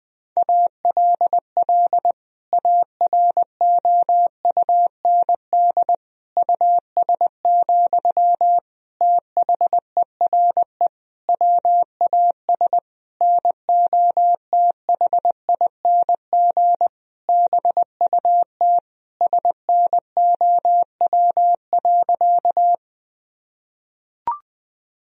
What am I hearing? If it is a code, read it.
Morse code, heard as a keyed tone, about 20 wpm, 700 Hz. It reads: ALL AROUND US, THERE WAS NOTHING BUT SNOW.
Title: Sentences from Top 1000 Words - Review Part 3 - 20wpm